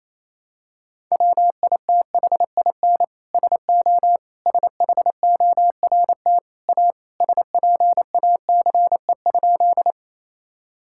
Morse code, heard as a keyed tone, about 28 wpm, 700 Hz. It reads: WITHIN SO SHORT A SPACE?
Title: Sentences from Top 1000 Words - Review Part 5 - 28wpm